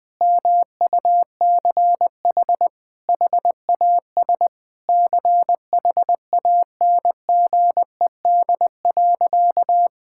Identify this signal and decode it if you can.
Morse code, heard as a keyed tone, about 21 wpm, 700 Hz. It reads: MUCH HAS CHANGED.
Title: Sentences with 3-Word Phrases - 20wpm